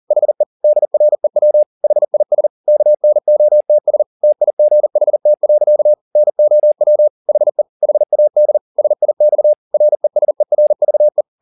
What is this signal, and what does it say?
Morse code, heard as a keyed tone, about 40 wpm, 600 Hz. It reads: HE DREW HIS KNOTS TIGHT. NOW HE HAD SIX RESERVE